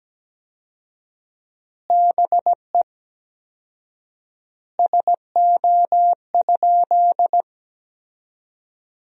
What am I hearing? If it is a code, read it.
Morse code, heard as a keyed tone, about 17 wpm, 700 Hz. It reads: BE SO?
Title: 4x Word-Spacing - Sentences from Top 300 Words - 17wpm